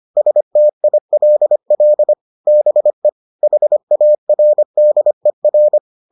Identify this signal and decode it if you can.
Morse code, heard as a keyed tone, about 25 wpm, 600 Hz. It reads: STILL BE HARDER